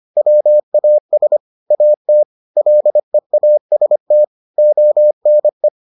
Morse code, heard as a keyed tone, about 25 wpm, 600 Hz. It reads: WAS AT LEAST ONE